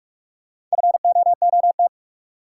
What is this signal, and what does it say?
Morse code, heard as a keyed tone, about 46 wpm, 700 Hz. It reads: FOOT
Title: Sentences from Top 500 Words - New Word Drill - 45wpm